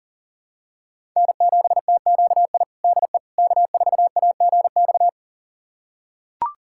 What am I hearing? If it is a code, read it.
Morse code, heard as a keyed tone, about 40 wpm, 700 Hz. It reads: N7TQI DE K4AGX
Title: USA Callsign DE Callsign 40wpm